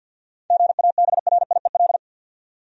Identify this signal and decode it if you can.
Morse code, heard as a keyed tone, about 51 wpm, 700 Hz. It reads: GABRIEL
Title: USA Common Names - 50wpm